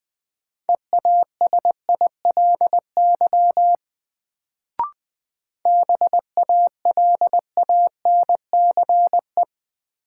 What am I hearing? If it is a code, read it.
Morse code, heard as a keyed tone, about 20 wpm, 700 Hz. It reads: EASILY BALANCE
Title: Sentences from Top 900 Words - New Word Drill - 20wpm